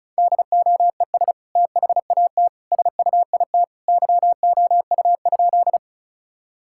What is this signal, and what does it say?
Morse code, heard as a keyed tone, about 36 wpm, 700 Hz. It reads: DOES THAT SUIT YOU?
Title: Sentences from Top 1600 Words 35wpm